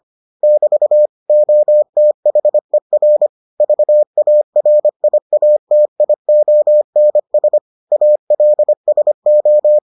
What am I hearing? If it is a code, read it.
Morse code, heard as a keyed tone, about 25 wpm, 600 Hz. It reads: <BT> OTHER VARIATIONS ALSO